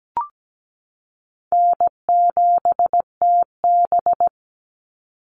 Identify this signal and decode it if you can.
Morse code, heard as a keyed tone, about 17 wpm, 700 Hz. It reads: N7TB
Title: Call Signs Repeated 3x - 17wpm